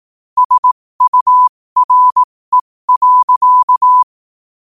Morse code, heard as a keyed tone, about 18 wpm, spaced as for 15 wpm, 1000 Hz. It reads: SURE.
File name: qotdfiles / 15wpm